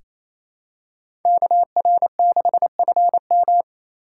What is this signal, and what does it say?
Morse code, heard as a keyed tone, about 28 wpm, 700 Hz. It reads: KR6FM
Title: USA Call Signs - 5 Characters - 2x Character Spacing - 28wpm